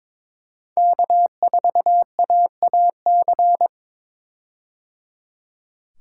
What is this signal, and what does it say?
Morse code, heard as a keyed tone, about 21 wpm, 700 Hz. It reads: K4AAC
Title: K1USN SST QSO Practice - 22wpm